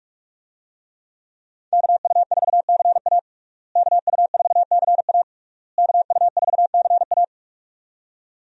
Morse code, heard as a keyed tone, about 41 wpm, 700 Hz. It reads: KU4CA KU4CA KU4CA
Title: USA Call Signs - 5 Characters - Repeated 3x - 45wpm